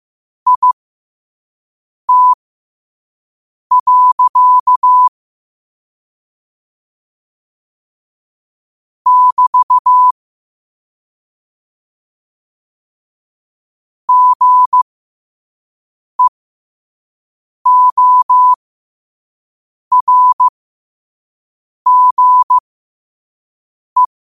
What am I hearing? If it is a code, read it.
Morse code, heard as a keyed tone, about 15 wpm, spaced as for 5 wpm, 1000 Hz. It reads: IT. <BT> GEORGE
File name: qotdfiles / 5wpm